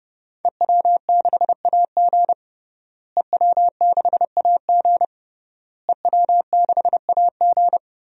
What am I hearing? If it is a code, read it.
Morse code, heard as a keyed tone, about 30 wpm, 700 Hz. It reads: EW6AG EW6AG EW6AG
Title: DXCC-100 Call Signs - 5 Characters - Repeated 3x - 30wpm